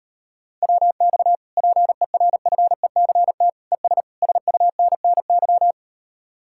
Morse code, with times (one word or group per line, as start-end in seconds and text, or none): 0.6-1.4 WX
1.6-3.5 PERFECT
3.7-4.0 ES
4.2-5.7 SUNNY